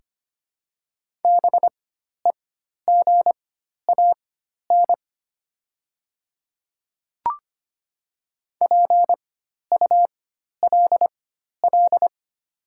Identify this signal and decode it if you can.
Morse code, heard as a keyed tone, about 25 wpm, spaced as for 12 wpm, 690 Hz. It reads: BEGAN PULL
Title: 4x Character Spacing - Top 500 Words - 25wpm